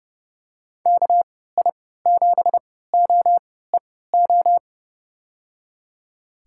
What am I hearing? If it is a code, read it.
Morse code, heard as a keyed tone, about 30 wpm, spaced as for 17 wpm, 700 Hz. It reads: KI7OEO
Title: USA Call Signs - 6 Characters - 3x Character Spacing - 30wpm